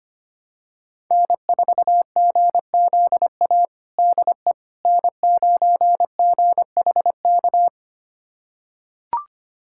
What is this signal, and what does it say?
Morse code, heard as a keyed tone, about 25 wpm, 700 Hz. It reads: N4GZA DE N9GHK
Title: USA Callsign DE Callsign - 5 Characters - 25wpm